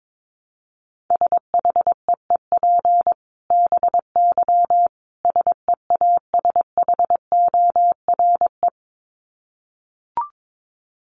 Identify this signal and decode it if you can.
Morse code, heard as a keyed tone, about 22 wpm, 700 Hz. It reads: SHEEP BY SEASHORE
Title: Dit Challenge 22wpm